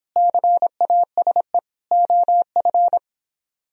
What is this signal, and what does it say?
Morse code, heard as a keyed tone, about 29 wpm, 700 Hz. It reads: CASE OF